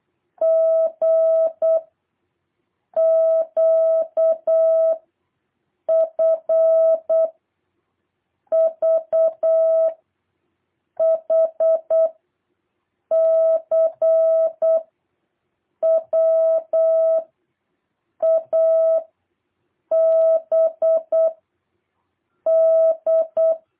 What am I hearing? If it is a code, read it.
Morse code, heard as a keyed tone, about 8 wpm, 650 Hz. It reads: G Q F V H C W A B D
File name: jh8cbh / 2023cw